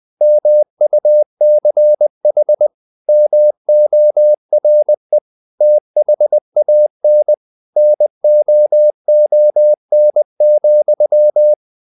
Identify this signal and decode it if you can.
Morse code, heard as a keyed tone, about 20 wpm, 600 Hz. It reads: MUCH MORE THAN NOON,
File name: the old man and the sea part 2 20wpm